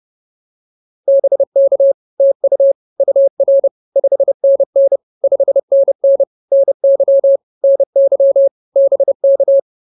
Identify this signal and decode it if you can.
Morse code, heard as a keyed tone, about 30 wpm, 550 Hz. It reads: BK TU UR 5NN 5NN NY NY BK
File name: POTA105 30